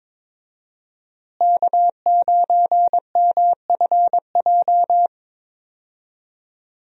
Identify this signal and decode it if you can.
Morse code, heard as a keyed tone, about 21 wpm, 700 Hz. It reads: K9MFJ
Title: USA Call Signs - 5 Characters - Repeated 3x - 22wpm